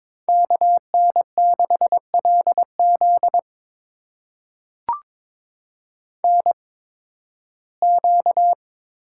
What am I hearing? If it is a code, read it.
Morse code, heard as a keyed tone, about 21 wpm, spaced as for 14 wpm, 700 Hz. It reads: KN6LZ NQ
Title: USA Call Signs - 8x Character Spacing - 22wpm